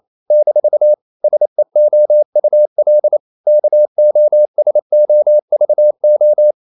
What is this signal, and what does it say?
Morse code, heard as a keyed tone, about 28 wpm, 600 Hz. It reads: - SEOUL KOSOVO
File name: CW / Practice / NationalCapitals28